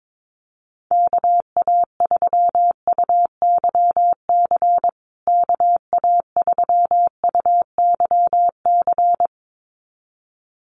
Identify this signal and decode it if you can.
Morse code, heard as a keyed tone, about 22 wpm, 700 Hz. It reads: KA3UYC KA3UYC
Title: USA Call Signs Repeated 2x 22wpm